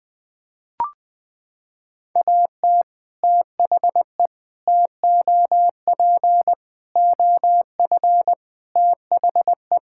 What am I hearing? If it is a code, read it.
Morse code, heard as a keyed tone, about 20 wpm, 700 Hz. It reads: AT THE TOP OF THE
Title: Sentences with 3-Word Phrases - 20wpm